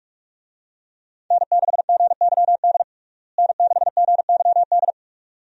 Morse code, heard as a keyed tone, about 43 wpm, 700 Hz. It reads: N6GYD N6GYD
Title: USA Call Signs - 5 Characters - Repeated 2x - 45wpm